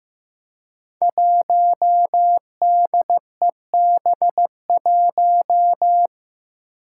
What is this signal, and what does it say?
Morse code, heard as a keyed tone, about 15 wpm, 700 Hz. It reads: 1DEB1
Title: Five Challenging Letter-Number Combinations 15wpm